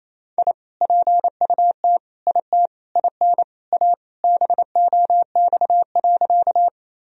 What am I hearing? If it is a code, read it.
Morse code, heard as a keyed tone, about 28 wpm, 700 Hz. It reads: I PUT IT IN A BOX.
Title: Sentences from Top 1000 Words - Review Part 2 - 28wpm